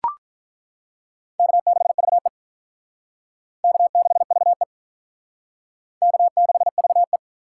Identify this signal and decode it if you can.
Morse code, heard as a keyed tone, about 42 wpm, 700 Hz. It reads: K6VE K6VE K6VE
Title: Speed Racing - USA 1x2 Call Signs 35wpm